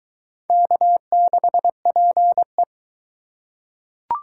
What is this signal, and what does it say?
Morse code, heard as a keyed tone, about 22 wpm, 700 Hz. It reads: K6PE